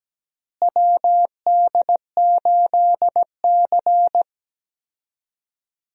Morse code, heard as a keyed tone, about 17 wpm, 700 Hz. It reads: WD8C
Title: USA Call Signs - 4 Characters - Repeated 2x - 17wpm